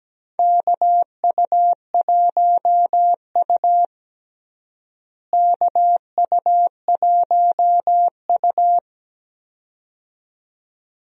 Morse code, heard as a keyed tone, about 17 wpm, 700 Hz. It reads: KU1U KU1U
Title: USA Call Signs - 4 Characters - Repeated 2x - 17wpm